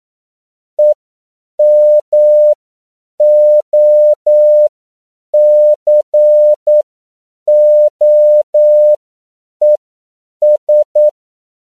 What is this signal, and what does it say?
Morse code, heard as a keyed tone, about 9 wpm, spaced as for 7 wpm, 600 Hz. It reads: EMOCOES
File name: CW / 10PPM